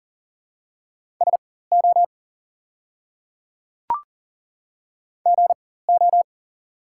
Morse code, heard as a keyed tone, about 40 wpm, spaced as for 19 wpm, 700 Hz. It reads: SO GO
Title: 4x Character Spacing - Top 100 Words - 40wpm